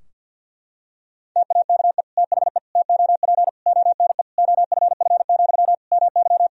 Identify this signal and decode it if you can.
Morse code, heard as a keyed tone, about 51 wpm, 700 Hz. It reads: TAKE THE TOP ONE OFF, MY
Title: Sentences from Top 1000 Words - Review Part 2 - 50wpm